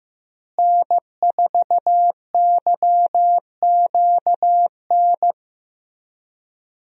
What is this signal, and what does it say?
Morse code, heard as a keyed tone, about 15 wpm, 700 Hz. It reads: N4YQN
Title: USA Call Signs - 5 Characters - Repeated 3x - 15wpm